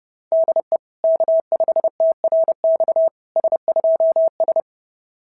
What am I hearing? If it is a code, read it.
Morse code, heard as a keyed tone, about 28 wpm, 660 Hz. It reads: DE K5TRX S2S
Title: SOTA QSO Practice 30wpm